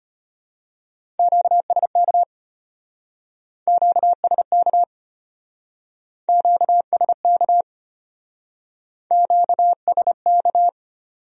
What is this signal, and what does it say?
Morse code, heard as a keyed tone, about 32 wpm, 700 Hz. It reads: QSK QSK QSK QSK